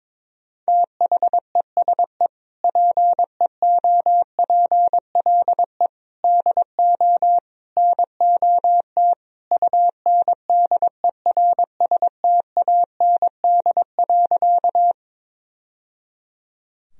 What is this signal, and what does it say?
Morse code, heard as a keyed tone, about 21 wpm, 700 Hz. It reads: THESE PEOPLE DO NOT UNDERSTAND.
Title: Sentences from Top 1000 Words - Review Part 1 - 22wpm